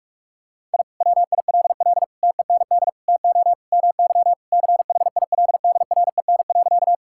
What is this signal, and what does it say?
Morse code, heard as a keyed tone, about 46 wpm, 690 Hz. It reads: I WILL TEND TO MY CHILDREN.
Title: Sentences from Top 1000 Words - Review Part 5 - 45wpm